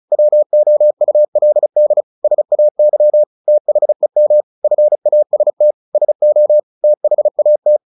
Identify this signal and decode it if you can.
Morse code, heard as a keyed tone, about 34 wpm, 600 Hz. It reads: WOULD SAY THEM FAST SO THAT
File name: the old man and the sea / the old man and the sea 1 35wpm